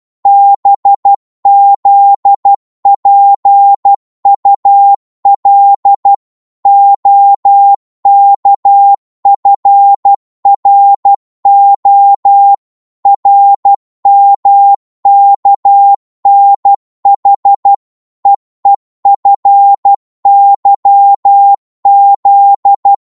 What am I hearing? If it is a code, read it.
Morse code, heard as a keyed tone, about 12 wpm, 800 Hz. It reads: BZPUL OKFRO RMKNH EEFYZ
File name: Morse / ltr-12wpm